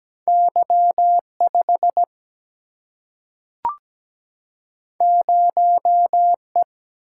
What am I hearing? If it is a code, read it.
Morse code, heard as a keyed tone, about 17 wpm, 700 Hz. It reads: Y5 0E
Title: Single-Letter Single-Number 17wpm